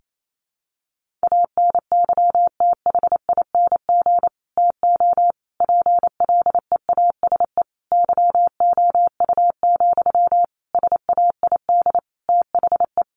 Morse code, heard as a keyed tone, about 28 wpm, 700 Hz. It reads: ANYTHING TO PLEASE YOU, SAID THE
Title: Sentences from Top 1500 Words 28wpm